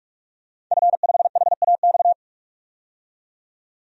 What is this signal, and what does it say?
Morse code, heard as a keyed tone, about 41 wpm, 700 Hz. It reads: F5HAX
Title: DXCC-100 Call Signs - 5 Characters - 45wpm